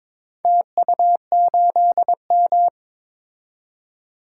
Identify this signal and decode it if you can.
Morse code, heard as a keyed tone, about 22 wpm, 700 Hz. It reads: TU8M
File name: practice / 22wpm / Four Letter-Number Combinations - 22wpm